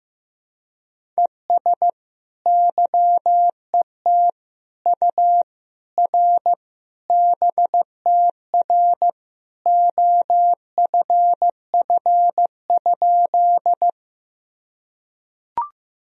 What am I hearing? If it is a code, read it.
Morse code, heard as a keyed tone, about 15 wpm, 700 Hz. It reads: ES YET U R BTR OFF?
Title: Sentences from Top 500 Words and Abbreviations 15wpm